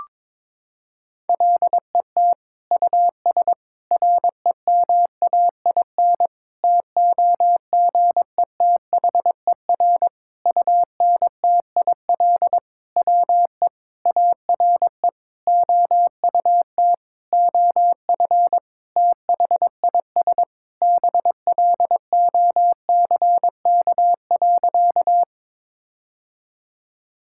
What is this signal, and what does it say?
Morse code, heard as a keyed tone, about 22 wpm, 700 Hz. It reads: LET US REMAIN TOGETHER UNTIL WE ARE OUT OF THIS BLOCK.
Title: Sentences from Top 1600 Words 22wpm